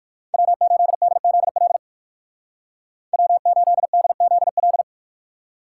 Morse code, heard as a keyed tone, about 54 wpm, 700 Hz. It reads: W8DZL W8DZL
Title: Speed Racing - USA 1x3 Call Signs 45wpm